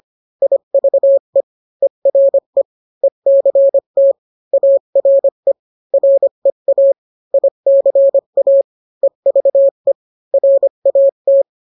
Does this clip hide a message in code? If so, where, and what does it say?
Yes: Morse code from 0.4 to 11.4 s, heard as a keyed tone, about 25 wpm, 550 Hz. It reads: IVE ERE ECT ARE REA ICA EVE RAT